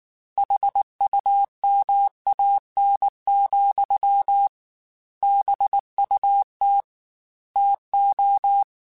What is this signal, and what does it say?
Morse code, heard as a keyed tone, about 19 wpm, 800 Hz. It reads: HUMAN, BUT TO